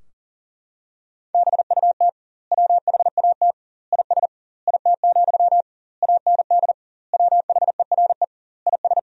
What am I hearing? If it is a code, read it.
Morse code, heard as a keyed tone, about 40 wpm, 700 Hz. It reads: BUT WHAT IS IT, AND WHERE IS
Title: 2x Word-Spacing - Sentences from Top 300 Words - 40wpm